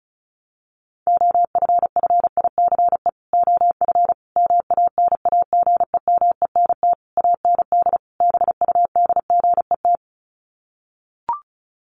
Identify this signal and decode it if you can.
Morse code, heard as a keyed tone, about 34 wpm, 700 Hz. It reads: OFFICE OF MANAGEMENT AND BUDGET
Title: Sets of 5 Words Encore2 35wpm